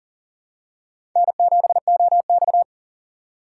Morse code, heard as a keyed tone, about 38 wpm, 700 Hz. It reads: N7OX
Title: USA Call Signs - 4x Character Spacing - 40wpm